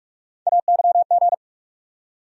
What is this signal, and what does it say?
Morse code, heard as a keyed tone, about 45 wpm, 700 Hz. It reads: AYG
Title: USA Call Sign Postfixes - 3 Letters 45wpm